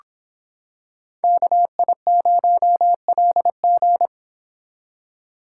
Morse code, heard as a keyed tone, about 25 wpm, 700 Hz. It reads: KI0LG